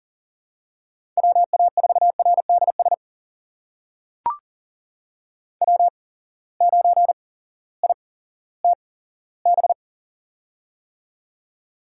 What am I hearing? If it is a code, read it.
Morse code, heard as a keyed tone, about 38 wpm, spaced as for 18 wpm, 700 Hz. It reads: WA4RDS W9ITB
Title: USA Call Signs - 8x Character Spacing - 40wpm